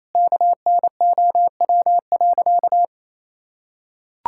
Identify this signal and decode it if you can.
Morse code, heard as a keyed tone, about 28 wpm, 700 Hz. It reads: KNOW.
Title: Sentences from Top 1000 Words - Review Part 6 - 28wpm